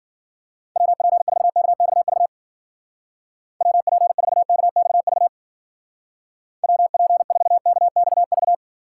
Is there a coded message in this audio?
Yes: Morse code from 0.8 to 8.5 s, heard as a keyed tone, about 51 wpm, 700 Hz. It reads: WP4KXV WP4KXV WP4KXV